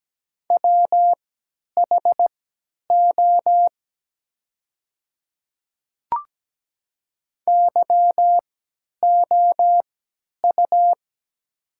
Morse code, heard as a keyed tone, about 17 wpm, spaced as for 10 wpm, 700 Hz. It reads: WHO YOU
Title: 3x Character Spacing - Top 100 Words - 17wpm